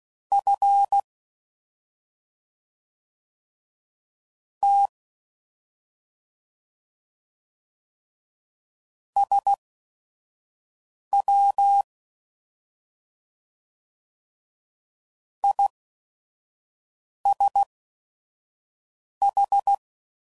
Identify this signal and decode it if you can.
Morse code, heard as a keyed tone, about 16 wpm, spaced as for 3 wpm, 800 Hz. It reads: FT SWISH